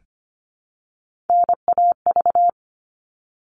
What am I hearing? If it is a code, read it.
Morse code, heard as a keyed tone, about 25 wpm, 700 Hz. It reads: NAV
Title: All Cut Numbers - 3-Digit 25wpm